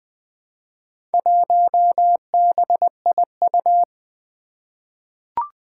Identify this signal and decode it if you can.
Morse code, heard as a keyed tone, about 20 wpm, 700 Hz. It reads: 1BIU
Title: Four Challenging Letter-Number Combinations 20wpm